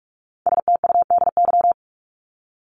Morse code, heard as a keyed tone, about 44 wpm, 700 Hz. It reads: STUDY
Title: Top 500 Words 45wpm